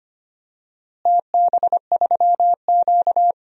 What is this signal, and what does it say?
Morse code, heard as a keyed tone, about 25 wpm, 700 Hz. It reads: TB3Q
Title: DXCC-100 Call Signs - 4 Characters - Repeated 2x - 25wpm